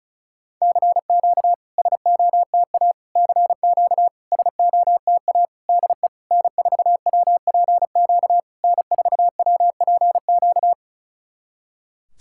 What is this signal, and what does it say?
Morse code, heard as a keyed tone, about 36 wpm, 690 Hz. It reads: CQ SOTA CQ SOTA DE N4WPQ N4WPQ
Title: SOTA QSO Practice 35wpm